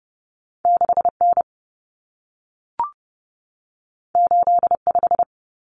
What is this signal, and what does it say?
Morse code, heard as a keyed tone, about 30 wpm, 700 Hz. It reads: 6N 85
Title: Common Cut Numbers - 2-Digit 30wpm